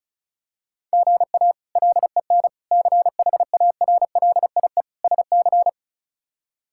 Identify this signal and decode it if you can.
Morse code, heard as a keyed tone, about 34 wpm, 700 Hz. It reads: GA LEN CHARLIE SC
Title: K1USN SST QSO Practice - 35wpm